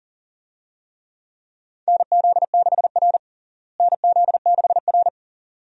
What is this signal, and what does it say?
Morse code, heard as a keyed tone, about 40 wpm, 700 Hz. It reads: NZ6R NZ6R
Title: USA Call Signs - 4 Characters - Repeated 2x - 40wpm